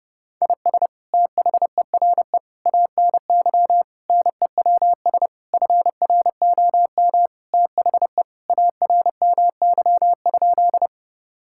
Morse code, heard as a keyed tone, about 32 wpm, 700 Hz. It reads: IS THERE ANY NEWS FROM THE ARMY?
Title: Sentences from Top 1000 Words - Review Part 5 - 30wpm